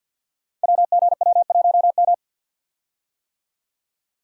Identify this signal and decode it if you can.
Morse code, heard as a keyed tone, about 53 wpm, 700 Hz. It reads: WGW1M
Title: Code Groups - 5 Letter:Numbers - 50wpm